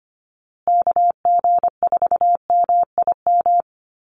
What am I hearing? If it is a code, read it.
Morse code, heard as a keyed tone, about 25 wpm, 700 Hz. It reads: KG4MIM
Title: USA Call Signs Repeated 3x 25wpm